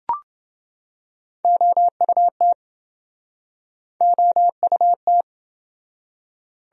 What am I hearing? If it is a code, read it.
Morse code, heard as a keyed tone, about 27 wpm, 700 Hz. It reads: OUT OUT